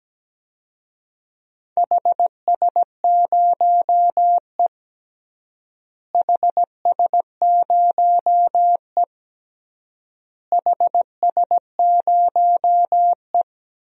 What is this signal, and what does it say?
Morse code, heard as a keyed tone, about 17 wpm, 700 Hz. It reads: HS0E HS0E HS0E
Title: DXCC-100 Call Signs - 4 Characters - Repeated 3x - 17wpm